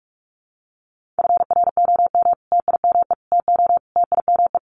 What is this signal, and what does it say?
Morse code, heard as a keyed tone, about 44 wpm, 700 Hz. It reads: FROM TIME TO TIME
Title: Sets of 4 Words 45wpm